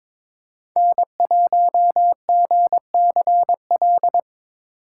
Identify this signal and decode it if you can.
Morse code, heard as a keyed tone, about 22 wpm, 700 Hz. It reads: N1GCL
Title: USA Call Signs - 5 Characters - 4x Character Spacing - 22wpm